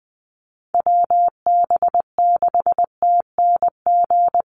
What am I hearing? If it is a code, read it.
Morse code, heard as a keyed tone, about 20 wpm, 700 Hz. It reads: WB6TNG
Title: USA Call Signs 20wpm